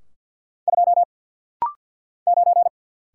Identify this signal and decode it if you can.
Morse code, heard as a keyed tone, about 48 wpm, 700 Hz. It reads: 2 9
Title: Single Letter-Number - Rapid-Fire 50wpm